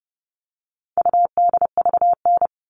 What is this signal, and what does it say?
Morse code, heard as a keyed tone, about 29 wpm, 700 Hz. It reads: UDVN
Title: All Cut Numbers - 4-Digit 30wpm